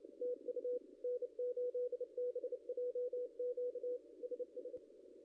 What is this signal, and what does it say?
Morse code, heard as a keyed tone, about 30 wpm, 500 Hz, with radio noise. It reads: TU N8BJQ SS
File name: contests / ss-cw-k5tr-op-n5rz -0600